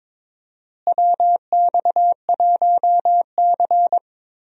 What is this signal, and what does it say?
Morse code, heard as a keyed tone, about 22 wpm, 700 Hz. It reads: WX1C
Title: USA Call Signs - 4 Characters - Repeated 2x - 22wpm